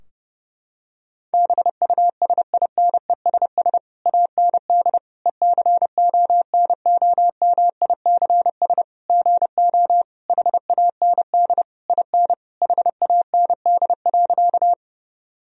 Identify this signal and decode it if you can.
Morse code, heard as a keyed tone, about 30 wpm, 700 Hz. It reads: BUSINESS AND ECONOMICS GO HAND IN HAND.
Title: Sentences from Top 1000 Words - Review Part 3 - 30wpm